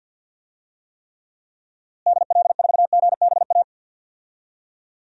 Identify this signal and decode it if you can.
Morse code, heard as a keyed tone, about 51 wpm, 700 Hz. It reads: DL4GBA
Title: DXCC-100 Call Signs - 6 Characters - Repeated 3x - 50wpm